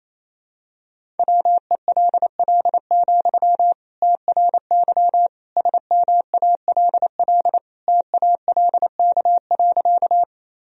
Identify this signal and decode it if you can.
Morse code, heard as a keyed tone, about 29 wpm, 700 Hz. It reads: WELL, TRY SMALL TALK.
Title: Sentences from Top 1000 Words - Review Part 1 - 28wpm